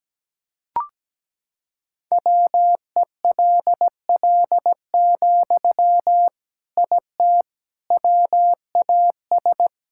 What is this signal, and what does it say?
Morse code, heard as a keyed tone, about 17 wpm, 700 Hz. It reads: WELL, IT WAS
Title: Sentences from Top 1000 Words - Review Part 6 - 17wpm